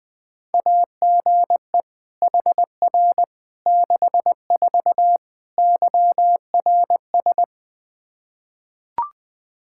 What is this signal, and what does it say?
Morse code, heard as a keyed tone, about 20 wpm, 700 Hz. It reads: AGE HR 64 YRS
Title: K1USN SST QSO Practice 20wpm